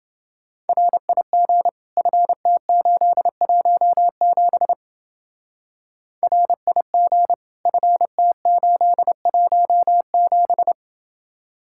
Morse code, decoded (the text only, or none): RIG FT817 RIG FT817